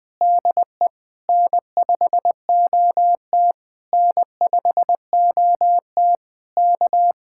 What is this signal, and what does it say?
Morse code, heard as a keyed tone, about 20 wpm, 700 Hz. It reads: DE N5OT N5OT K
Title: Speed-Racing Common QSO Phrases - 17wpm